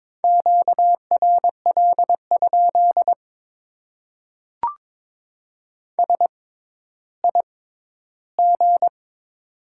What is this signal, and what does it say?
Morse code, heard as a keyed tone, about 22 wpm, spaced as for 12 wpm, 700 Hz. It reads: QRL? SIG
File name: Top 90 QSO Elements - 6x Character Spacing - 22wpm